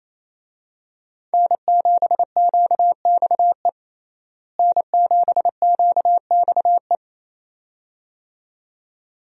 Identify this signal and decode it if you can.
Morse code, heard as a keyed tone, about 28 wpm, 700 Hz. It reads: N7QXE N7QXE
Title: USA Call Signs - 5 Characters - Repeated 2x - 28wpm